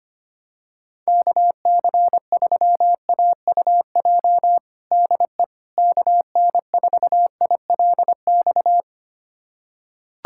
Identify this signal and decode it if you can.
Morse code, heard as a keyed tone, about 25 wpm, 700 Hz. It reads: KC3AUJ DE KN4ILX
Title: USA Callsign DE Callsign - 6 Characters - 25wpm